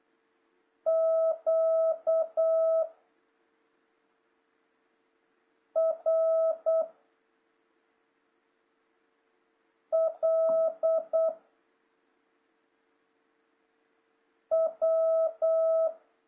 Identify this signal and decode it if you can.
Morse code, heard as a keyed tone, about 8 wpm, 660 Hz. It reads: Q R L W
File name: jh8cbh / 2026cw